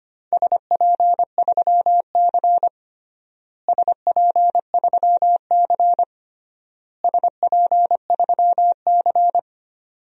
Morse code, heard as a keyed tone, about 25 wpm, 700 Hz. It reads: SP3C SP3C SP3C
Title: DXCC-100 Call Signs - 4 Characters - Repeated 3x - 25wpm